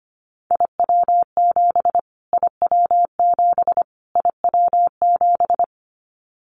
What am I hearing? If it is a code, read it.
Morse code, heard as a keyed tone, about 25 wpm, 700 Hz. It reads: IW7 IW7 IW7
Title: Callsign Prefixes for Top 25 DXCC - Repeated 3x - 25wpm